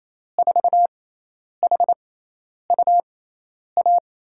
Morse code, heard as a keyed tone, about 28 wpm, spaced as for 10 wpm, 700 Hz. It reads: 4HUA
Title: USA Call Signs - 6x Character Spacing - 28wpm